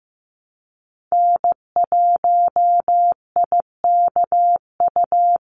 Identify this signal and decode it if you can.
Morse code, heard as a keyed tone, about 15 wpm, 700 Hz. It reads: N1IKU
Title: USA Call Signs 15wpm